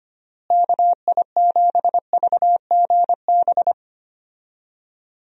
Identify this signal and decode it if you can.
Morse code, heard as a keyed tone, about 25 wpm, 700 Hz. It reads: KI7VGB
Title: USA Call Signs - 6 Characters - Repeated 2x - 25wpm